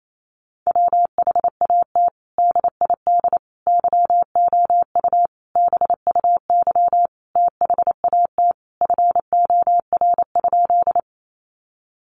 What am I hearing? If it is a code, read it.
Morse code, heard as a keyed tone, about 28 wpm, 700 Hz. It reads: WHAT DID YOU BUY THAT FOR?
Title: Sentences from Top 700 Words 28wpm